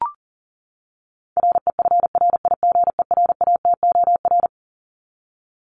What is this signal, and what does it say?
Morse code, heard as a keyed tone, about 40 wpm, 700 Hz. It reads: REFRIGERATOR
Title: Top 1501-1750 Words 40wpm